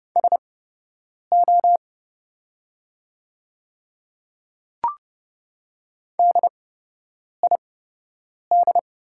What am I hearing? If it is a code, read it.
Morse code, heard as a keyed tone, about 30 wpm, spaced as for 8 wpm, 700 Hz. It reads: SO DID
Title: 8x Character Spacing - Top 100 Words - 30wpm